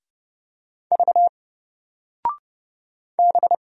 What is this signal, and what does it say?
Morse code, heard as a keyed tone, about 30 wpm, 700 Hz. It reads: V B
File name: Single-Letter-Number-30wpm